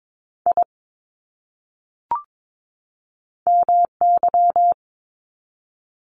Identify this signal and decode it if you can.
Morse code, heard as a keyed tone, about 22 wpm, 700 Hz. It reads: I MY
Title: Top 500 Words 22wpm